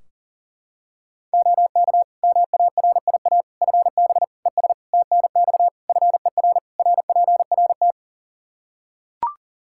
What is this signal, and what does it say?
Morse code, heard as a keyed tone, about 40 wpm, 700 Hz. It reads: OK MARIA FB ES TNX FER RPRT
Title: Practice QSO - Exchange 2 - Info - 40wpm